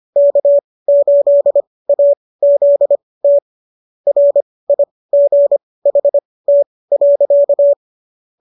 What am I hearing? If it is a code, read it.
Morse code, heard as a keyed tone, about 25 wpm, spaced as for 18 wpm, 590 Hz. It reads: K8AZT RIGHT.